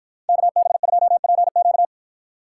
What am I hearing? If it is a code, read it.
Morse code, heard as a keyed tone, about 53 wpm, 700 Hz. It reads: KB2PQ